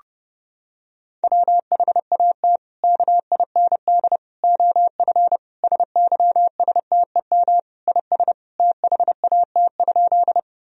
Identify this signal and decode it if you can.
Morse code, heard as a keyed tone, about 30 wpm, 700 Hz. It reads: WHAT KIND OF SYSTEM IS THAT?
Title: Sentences from Top 1000 Words - Review Part 3 - 30wpm